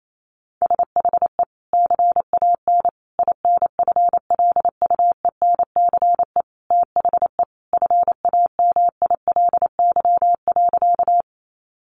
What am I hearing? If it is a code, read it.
Morse code, heard as a keyed tone, about 28 wpm, 700 Hz. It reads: SHE CAN INFLUENCE THE FAMILY.
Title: Sentences from Top 1100 Words 28wpm